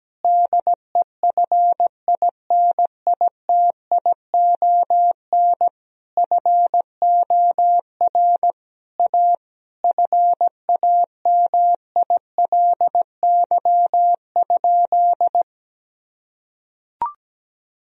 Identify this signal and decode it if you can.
Morse code, heard as a keyed tone, about 17 wpm, 700 Hz. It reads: DEFINITION FOR A FAMILY?
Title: Sentences from Top 1000 Words - Review Part 4 - 17wpm